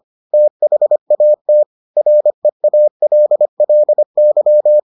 Morse code, heard as a keyed tone, about 25 wpm, 600 Hz. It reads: THAT REALLY